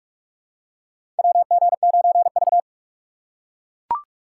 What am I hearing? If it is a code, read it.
Morse code, heard as a keyed tone, about 42 wpm, 690 Hz. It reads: WG9V
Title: USA Call Signs - 4 Characters - 8x Character Spacing - 45wpm